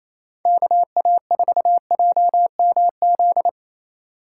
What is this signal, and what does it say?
Morse code, heard as a keyed tone, about 28 wpm, 700 Hz. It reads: KA4JMZ
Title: USA Call Signs - 6 Characters - 4x Character Spacing - 28wpm